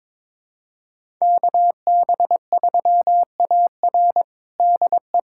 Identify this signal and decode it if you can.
Morse code, heard as a keyed tone, about 22 wpm, 700 Hz. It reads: KB3AR DE
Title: USA Callsign DE Callsign - 5 Characters - 22wpm